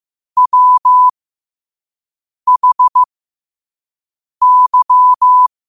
Morse code, heard as a keyed tone, about 15 wpm, spaced as for 5 wpm, 1000 Hz. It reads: WHY
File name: qotdfiles / 5wpm